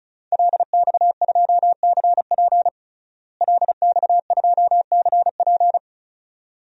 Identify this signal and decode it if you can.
Morse code, heard as a keyed tone, about 34 wpm, 700 Hz. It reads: LX2CP LX2CP
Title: DXCC-100 Call Signs - 5 Characters - Repeated 2x - 35wpm